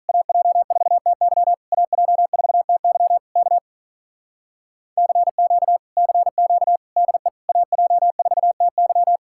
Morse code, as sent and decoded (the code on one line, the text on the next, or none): .- .--- ....- - -.-- / .- .--- ....- - -.-- / -.- / -.-. --.- / -.-. --.- / -.. / .- .--- ....- - -.--
AJ4TY AJ4TY K CQ CQ DE AJ4TY